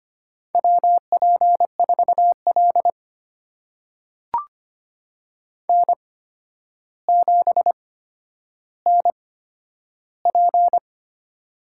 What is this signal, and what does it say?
Morse code, heard as a keyed tone, about 25 wpm, spaced as for 11 wpm, 700 Hz. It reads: WP4L N7NP
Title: USA Call Signs - 4 Characters - 8x Character Spacing - 25wpm